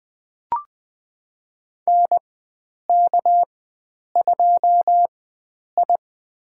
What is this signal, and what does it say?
Morse code, heard as a keyed tone, about 21 wpm, spaced as for 9 wpm, 700 Hz. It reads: NK2I
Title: USA Call Signs - 4 Characters - 4x Character Spacing - 20wpm